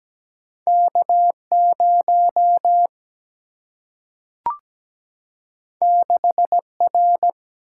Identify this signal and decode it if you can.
Morse code, heard as a keyed tone, about 17 wpm, 700 Hz. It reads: K0 6R
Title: Single-Letter Single-Number 17wpm